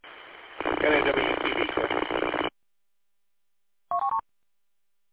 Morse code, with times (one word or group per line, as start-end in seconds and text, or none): none